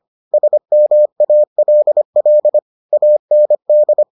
Morse code, heard as a keyed tone, about 25 wpm, 600 Hz. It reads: SMALL AND